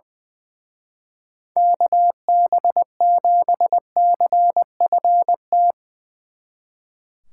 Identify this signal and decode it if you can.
Morse code, heard as a keyed tone, about 22 wpm, 700 Hz. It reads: KB7CFT
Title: Speed Racing - USA 2x3 Call Signs 20wpm